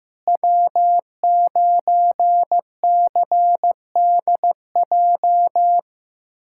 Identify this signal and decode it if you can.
Morse code, heard as a keyed tone, about 15 wpm, 700 Hz. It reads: W9CDJ
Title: USA Callsign DE Callsign - 5 Characters - 15wpm